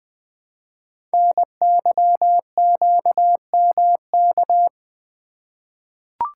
Morse code, heard as a keyed tone, about 20 wpm, 700 Hz. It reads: NYQMK
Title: Code Groups - 5 Letters - 20wpm